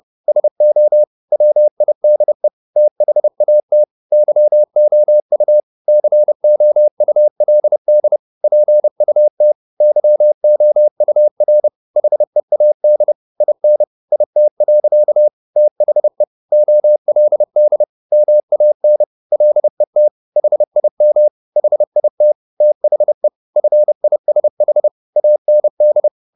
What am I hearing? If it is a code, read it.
Morse code, heard as a keyed tone, about 30 wpm, 600 Hz. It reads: SO WIDE THAT YOU COULD PUT YOUR HEAD IN IT. THE OLD MAN LET HIM HIT THE FISH AND